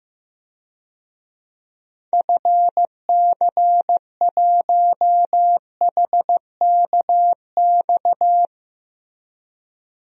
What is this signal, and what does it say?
Morse code, heard as a keyed tone, about 15 wpm, 700 Hz. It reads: FC1HKX
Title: DXCC-100 Call Signs - 6 Characters - 15wpm